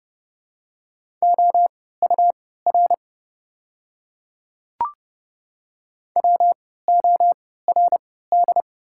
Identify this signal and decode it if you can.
Morse code, heard as a keyed tone, about 29 wpm, spaced as for 17 wpm, 690 Hz. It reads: OUR WORD
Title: 3x Character Spacing - Top 200 Words - 30wpm